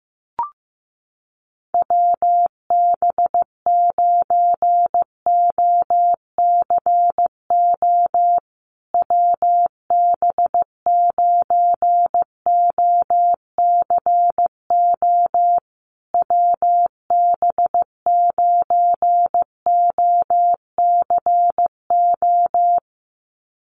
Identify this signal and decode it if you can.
Morse code, heard as a keyed tone, about 15 wpm, 700 Hz. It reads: WB9OCO WB9OCO WB9OCO
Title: USA Call Signs Repeated 3x 15wpm